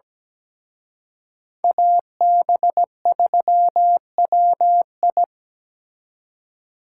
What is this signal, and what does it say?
Morse code, heard as a keyed tone, about 20 wpm, 700 Hz. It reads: AB3WI
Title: Speed Racing - USA 2x2 Call Signs 17wpm